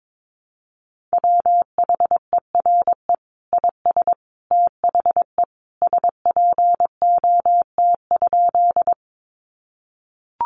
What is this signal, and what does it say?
Morse code, heard as a keyed tone, about 22 wpm, 700 Hz. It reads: WHERE IS THE SPOT?
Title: Sentences from Top 1100 Words 22wpm